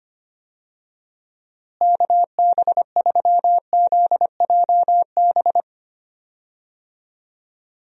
Morse code, heard as a keyed tone, about 25 wpm, 700 Hz. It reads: KB3ZJB
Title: USA Call Signs - 6 Characters - Repeated 3x - 25wpm